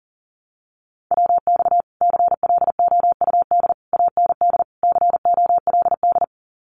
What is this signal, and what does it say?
Morse code, heard as a keyed tone, about 40 wpm, 700 Hz. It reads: WX CLOUD AND COLD
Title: Common QSO Phrases Encore 40wpm